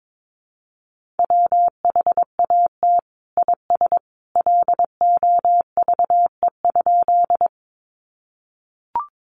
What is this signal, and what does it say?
Morse code, heard as a keyed tone, about 22 wpm, 700 Hz. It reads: WHAT IS LOVE?
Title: Sentences from Top 400 Words 22wpm